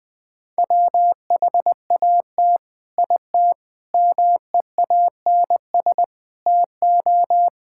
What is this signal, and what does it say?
Morse code, heard as a keyed tone, about 20 wpm, 700 Hz. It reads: WHAT IT MEANS TO
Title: Build A Phrase - 5-Word Phrases - 20wpm